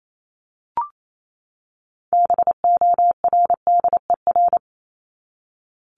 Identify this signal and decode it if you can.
Morse code, heard as a keyed tone, about 28 wpm, 700 Hz. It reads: BORDER
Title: Top 1751-2000 Words 28wpm